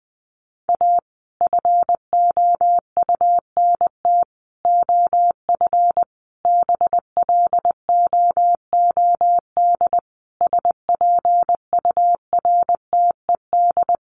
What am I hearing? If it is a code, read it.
Morse code, heard as a keyed tone, about 20 wpm, 700 Hz. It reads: A FOUNT OF BLOOD SPURTED
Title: A Sound Of Thunder 20wpm